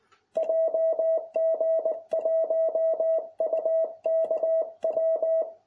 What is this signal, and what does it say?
Morse code, heard as a keyed tone, about 20 wpm, 660 Hz. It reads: JG1UKW